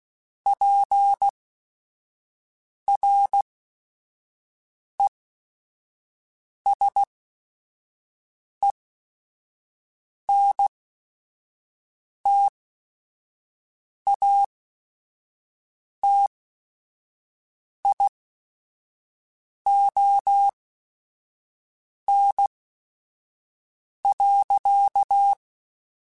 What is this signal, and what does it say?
Morse code, heard as a keyed tone, about 16 wpm, spaced as for 5 wpm, 800 Hz. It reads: PRESENTATION.
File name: Quotes / Quotes-mp3s / Persperation @5